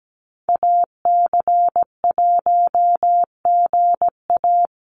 Morse code, heard as a keyed tone, about 17 wpm, 700 Hz. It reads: AC1GA
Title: USA Call Signs Repeated 2x 17wpm